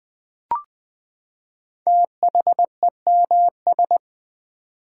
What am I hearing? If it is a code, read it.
Morse code, heard as a keyed tone, about 20 wpm, 700 Hz. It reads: THEMS
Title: Build A Word - 5 Letters At A Time - 10-Letter Words - 20wpm